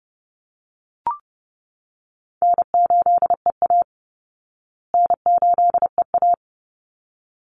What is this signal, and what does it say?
Morse code, heard as a keyed tone, about 30 wpm, 700 Hz. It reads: N8EA N8EA
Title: Easy Call Signs Repeated 2x - 30wpm